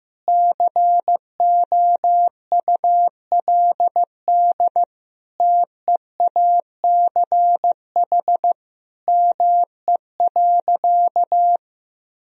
Morse code, heard as a keyed tone, about 15 wpm, 700 Hz. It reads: COULD TEACH ME.
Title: Sentences from Top 1000 Words - Review Part 2 - 15wpm